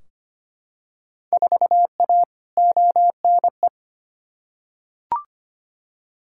Morse code, heard as a keyed tone, about 25 wpm, 700 Hz. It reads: 4A ONE
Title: Field Day Exchanges Encore - 25wpm